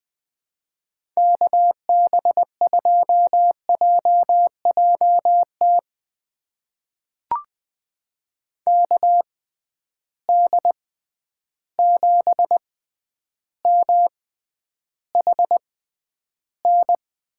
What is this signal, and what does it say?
Morse code, heard as a keyed tone, about 20 wpm, spaced as for 10 wpm, 700 Hz. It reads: KB2JJT KD7MHN
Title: USA Call Signs - 6 Characters - 6x Character Spacing - 20wpm